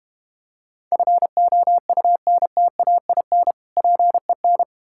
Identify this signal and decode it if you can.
Morse code, heard as a keyed tone, about 31 wpm, 690 Hz. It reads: FOUNTAIN PEN
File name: Easy Words - Speed-Racing with Hints - 25wpm